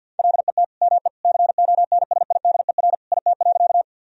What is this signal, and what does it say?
Morse code, heard as a keyed tone, about 50 wpm, 700 Hz. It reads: LET ME CONSIDER IT.
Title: Sentences from Top 1000 Words - Review Part 4 - 50wpm